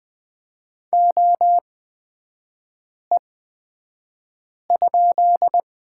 Morse code, heard as a keyed tone, about 20 wpm, spaced as for 5 wpm, 700 Hz. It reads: OE?
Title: 236 Lesson 38e - Letter q - Three Characters 20wpm